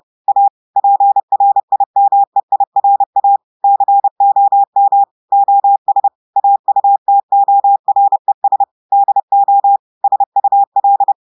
Morse code, heard as a keyed tone, about 30 wpm, 800 Hz. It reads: A PRIMEIRA COM OS AUTORES DO SUL
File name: audios / portugais / portugais texte en clair 1 vitesse 30 mots minutes